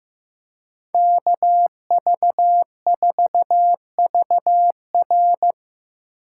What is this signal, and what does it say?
Morse code, heard as a keyed tone, about 15 wpm, 700 Hz. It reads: KV4VR
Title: USA Call Signs - 5 Characters - 6x Character Spacing - 15wpm